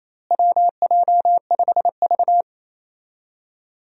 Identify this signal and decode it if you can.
Morse code, heard as a keyed tone, about 28 wpm, 700 Hz. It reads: WJ5V
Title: USA Call Signs - 4 Characters - Repeated 3x - 28wpm